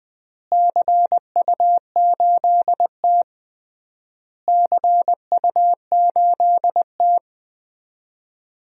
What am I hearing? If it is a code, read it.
Morse code, heard as a keyed tone, about 20 wpm, 700 Hz. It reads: CU8T CU8T
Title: DXCC-100 Call Signs - 4 Characters - Repeated 3x - 20wpm